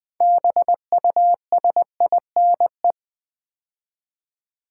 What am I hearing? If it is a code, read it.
Morse code, heard as a keyed tone, about 20 wpm, 700 Hz. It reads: BUSINE
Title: Build A Word - 2 Letters At A Time - 8-Letter Words - 20wpm